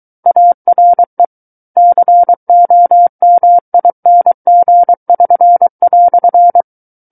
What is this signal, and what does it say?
Morse code, heard as a keyed tone, about 23 wpm, 700 Hz. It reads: ARE COMING<SN>"
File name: Book One / 14-Chapter 14 - 23 WPM